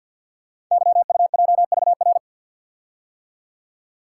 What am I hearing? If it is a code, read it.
Morse code, heard as a keyed tone, about 49 wpm, 700 Hz. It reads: YUJVR
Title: Code Groups - 5 Letters - 50wpm